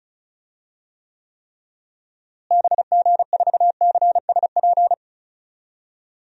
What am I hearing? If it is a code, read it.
Morse code, heard as a keyed tone, about 36 wpm, 700 Hz. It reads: BG4CSP
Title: DXCC-100 Call Signs - 6 Characters - Repeated 3x - 35wpm